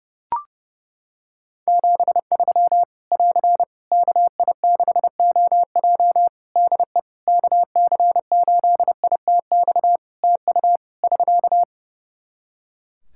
Morse code, heard as a keyed tone, about 30 wpm, 700 Hz. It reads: 73 <AR> KI6OJ DE KC8ITX TU <SK>
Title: K1USN SST QSO Practice 30wpm